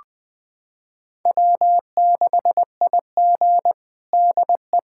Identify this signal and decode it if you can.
Morse code, heard as a keyed tone, about 20 wpm, 700 Hz. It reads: W6IG DE
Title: USA Callsign DE Callsign - 4 Characters - 20wpm